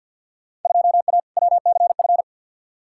Morse code, heard as a keyed tone, about 48 wpm, 700 Hz. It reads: 2A WCF
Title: Field Day Exchanges Encore - 50wpm